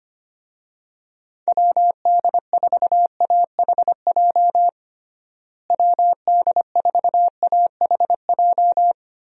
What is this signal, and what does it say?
Morse code, heard as a keyed tone, about 25 wpm, 700 Hz. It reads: WD4AHJ WD4AHJ
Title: USA Call Signs - 6 Characters - Repeated 2x - 25wpm